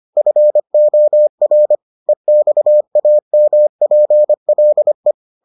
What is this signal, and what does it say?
Morse code, heard as a keyed tone, about 25 wpm, 600 Hz. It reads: FOR EXAMPLE